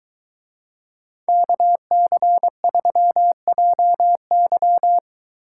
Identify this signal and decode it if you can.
Morse code, heard as a keyed tone, about 23 wpm, 700 Hz. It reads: KC3JY